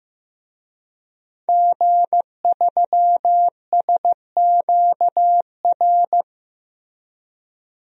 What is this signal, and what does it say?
Morse code, heard as a keyed tone, about 14 wpm, 700 Hz. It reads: G3SQR
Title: DXCC-100 Call Signs - 5 Characters - Repeated 2x - 15wpm